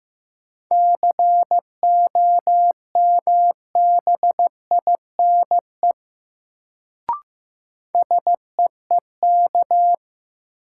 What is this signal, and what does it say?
Morse code, heard as a keyed tone, about 15 wpm, 700 Hz. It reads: COMBINE SEEK
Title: Sentences from Top 1600 Words - New Word Drill - 15wpm